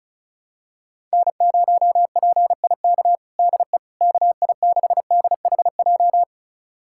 Morse code, heard as a keyed tone, about 34 wpm, 700 Hz. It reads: N0PIK DE KI6DHJ
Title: USA Callsign DE Callsign 35wpm